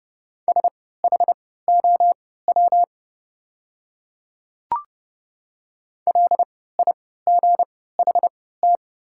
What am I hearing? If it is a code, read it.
Morse code, heard as a keyed tone, about 30 wpm, spaced as for 17 wpm, 700 Hz. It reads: SHOW LIGHT
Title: 3x Character Spacing - Top 300 Words - 30wpm